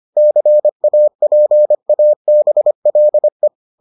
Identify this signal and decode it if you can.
Morse code, heard as a keyed tone, about 25 wpm, 600 Hz. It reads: CAPABLE